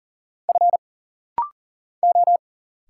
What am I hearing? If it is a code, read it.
Morse code, heard as a keyed tone, about 40 wpm, 700 Hz. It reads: F O
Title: Single Letters - Rapid-Fire 40wpm